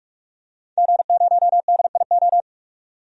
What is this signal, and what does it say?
Morse code, heard as a keyed tone, about 46 wpm, 700 Hz. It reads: G0DIO
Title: DXCC-100 Call Signs - 5 Characters - 45wpm